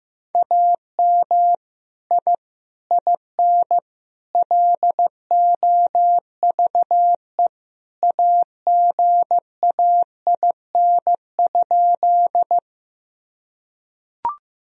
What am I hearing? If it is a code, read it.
Morse code, heard as a keyed tone, about 15 wpm, 700 Hz. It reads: AM I IN LOVE AGAIN?
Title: Sentences from Top 1000 Words - Review Part 2 - 15wpm